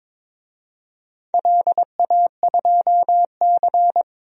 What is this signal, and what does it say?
Morse code, heard as a keyed tone, about 22 wpm, 700 Hz. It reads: LA2C
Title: DXCC-100 Call Signs - 4 Characters - 22wpm